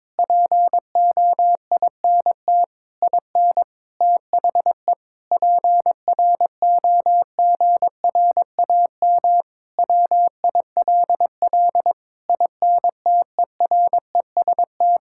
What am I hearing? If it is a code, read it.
Morse code, heard as a keyed tone, about 22 wpm, 700 Hz. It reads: POINT IN THE PROGRAM WILL INTEREST
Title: Sentences from Top 1000 Words - Review Part 5 - 22wpm